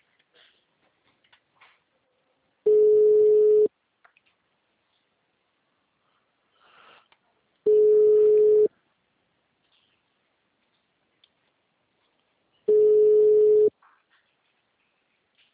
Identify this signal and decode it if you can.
Ringback tone (Europe, 425 Hz).
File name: out- -2006- - - .61174